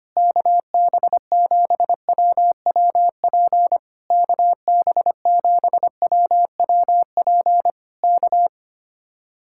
Morse code, heard as a keyed tone, about 25 wpm, 700 Hz. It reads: KB7WWP KB7WWP K
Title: Speed-Racing Common QSO Phrases - 25wpm